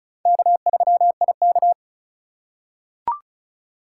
Morse code, heard as a keyed tone, about 34 wpm, 700 Hz. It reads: K3IK
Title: USA Call Signs - 4 Characters - 3x Character Spacing - 35wpm